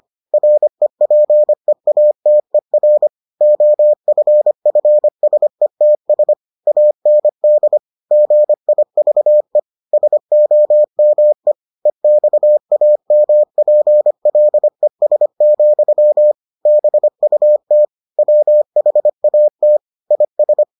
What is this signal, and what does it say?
Morse code, heard as a keyed tone, about 25 wpm, 600 Hz. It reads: REPEATER OFFSETS AND GIVE SOME EXAMPLES, BUT WHAT IS